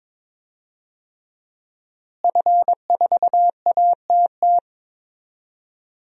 Morse code, heard as a keyed tone, about 22 wpm, 700 Hz. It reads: F4ATT
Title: DXCC-100 Call Signs - 5 Characters - Repeated 3x - 22wpm